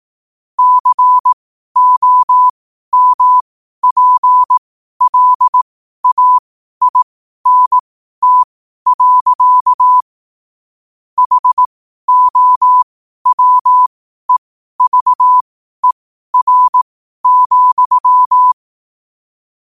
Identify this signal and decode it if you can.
Morse code, heard as a keyed tone, about 18 wpm, spaced as for 12 wpm, 1000 Hz. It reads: COMPLAINT. HOWEVER,
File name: qotdfiles / 12wpm